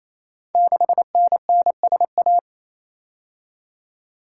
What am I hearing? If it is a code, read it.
Morse code, heard as a keyed tone, about 27 wpm, 700 Hz. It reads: 6NNSA
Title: Code Groups - 5 Letter:Numbers - 28wpm